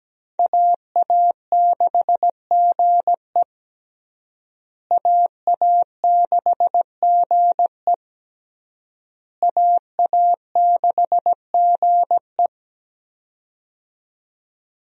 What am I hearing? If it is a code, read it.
Morse code, heard as a keyed tone, about 17 wpm, 700 Hz. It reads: AA6GE AA6GE AA6GE
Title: USA Call Signs - 5 Characters - Repeated 3x - 17wpm